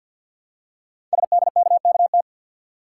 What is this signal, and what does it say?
Morse code, heard as a keyed tone, about 50 wpm, 700 Hz. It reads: SDKKT
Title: Code Groups - 5 Letters - 50wpm